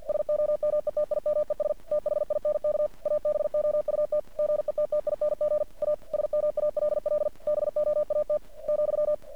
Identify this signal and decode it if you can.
Morse code, heard as a keyed tone, about 50 wpm, 600 Hz. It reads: SOMETIMES THINK ABOUT GETTING A SMALL BOAT ,
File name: archived / CRSnet